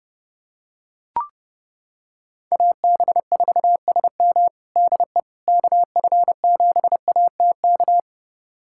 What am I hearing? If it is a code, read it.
Morse code, heard as a keyed tone, about 30 wpm, 700 Hz. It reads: AB4SM DE KF7ATK
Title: USA Callsign DE Callsign 30wpm